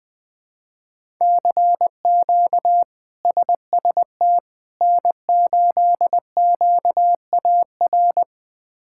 Morse code, heard as a keyed tone, about 20 wpm, 700 Hz. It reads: CQ SST N8QAR
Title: K1USN SST QSO Practice - 20wpm